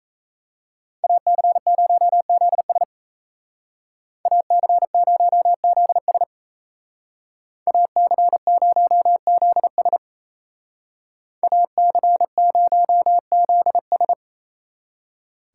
Morse code, heard as a keyed tone, about 37 wpm, 700 Hz. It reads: AC0ZS AC0ZS AC0ZS AC0ZS